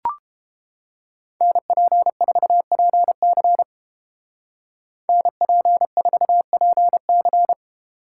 Morse code, heard as a keyed tone, about 31 wpm, 700 Hz. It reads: NP4PC NP4PC